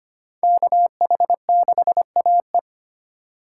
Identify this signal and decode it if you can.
Morse code, heard as a keyed tone, about 34 wpm, 700 Hz. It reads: KH6AE